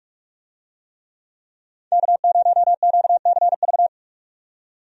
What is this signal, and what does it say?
Morse code, heard as a keyed tone, about 47 wpm, 700 Hz. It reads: K0QCV
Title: USA Call Signs - 5 Characters - 45wpm